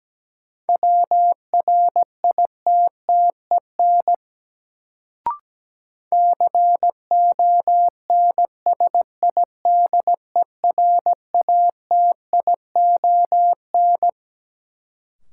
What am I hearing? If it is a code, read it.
Morse code, heard as a keyed tone, about 17 wpm, 700 Hz. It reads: WRITTEN CONSIDERATION
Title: Sentences from Top 1500 Words - New Word Drill - 17wpm